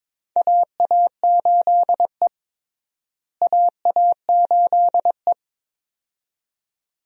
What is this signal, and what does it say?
Morse code, heard as a keyed tone, about 21 wpm, 700 Hz. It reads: AA8E AA8E
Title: USA Call Signs - 4 Characters - Repeated 2x - 22wpm